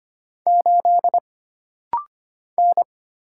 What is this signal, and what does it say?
Morse code, heard as a keyed tone, about 25 wpm, 700 Hz. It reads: 8 N
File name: Single Character (Letter:Number:Punctuation:Prosign) - Rapid-Fire - 25wpm